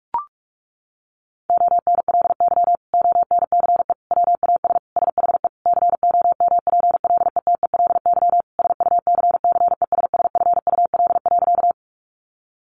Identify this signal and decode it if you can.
Morse code, heard as a keyed tone, about 44 wpm, 700 Hz. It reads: ONLY ONCE WAS SHE COMPLETELY SUCCESSFUL.
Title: Sentences from Top 1100 Words 45wpm